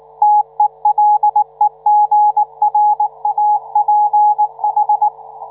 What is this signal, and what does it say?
Morse code, heard as a keyed tone, about 19 wpm, 840 Hz. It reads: TELEGRAPH